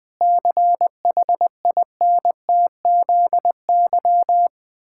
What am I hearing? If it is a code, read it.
Morse code, heard as a keyed tone, about 21 wpm, 700 Hz. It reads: CHINTZY
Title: Pangram Challenge v2 - 20wpm